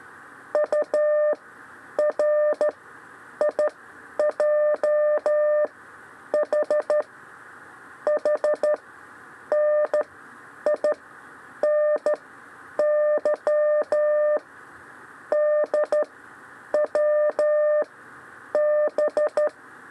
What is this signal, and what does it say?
Morse code, heard as a keyed tone, about 13 wpm, spaced as for 8 wpm, 600 Hz. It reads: URIJH HNINY DWB
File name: 6 lezione cw 2